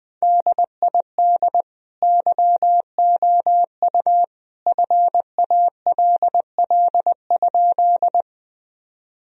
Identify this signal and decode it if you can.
Morse code, heard as a keyed tone, about 20 wpm, 700 Hz. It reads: DID YOU FALL?
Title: Sentences from Top 1000 Words - Review Part 2 - 20wpm